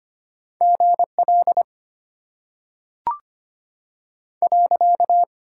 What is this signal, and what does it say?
Morse code, heard as a keyed tone, about 25 wpm, 700 Hz. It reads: G L .
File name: Top 90 QSO Elements - 6x Character Spacing - 25wpm